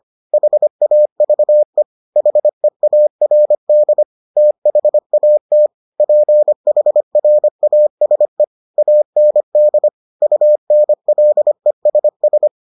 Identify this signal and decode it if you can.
Morse code, heard as a keyed tone, about 25 wpm, 600 Hz. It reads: HAVE HEARD THAT PHRASE AND UNLESS